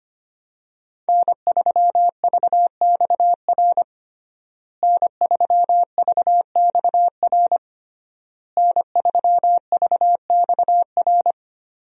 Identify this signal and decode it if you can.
Morse code, heard as a keyed tone, about 25 wpm, 700 Hz. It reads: N3VXR N3VXR N3VXR
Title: USA Call Signs - 5 Characters - Repeated 3x - 25wpm